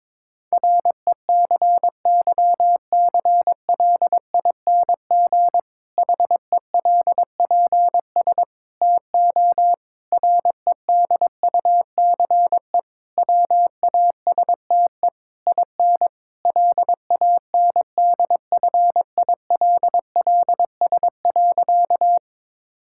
Morse code, heard as a keyed tone, about 22 wpm, 700 Hz. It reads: RECYCLING HELPS TO REDUCE WASTE IN LANDFILLS.
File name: Difficult Sentences with Hints - 22wpm